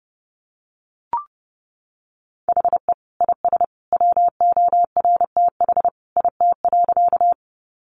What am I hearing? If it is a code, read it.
Morse code, heard as a keyed tone, about 30 wpm, 700 Hz. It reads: HE IS WORTH IT.
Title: Sentences from Top 800 Words 30wpm